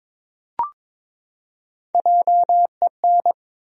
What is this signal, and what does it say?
Morse code, heard as a keyed tone, about 21 wpm, 700 Hz. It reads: JEN
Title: USA Common Names - 22wpm